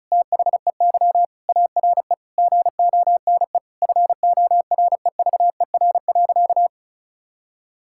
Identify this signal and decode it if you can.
Morse code, heard as a keyed tone, about 34 wpm, 700 Hz. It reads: THEY ARE GONE FOREVER.
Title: Sentences from Top 1600 Words 35wpm